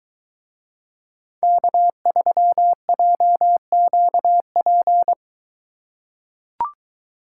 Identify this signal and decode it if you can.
Morse code, heard as a keyed tone, about 21 wpm, 700 Hz. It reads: K3JQP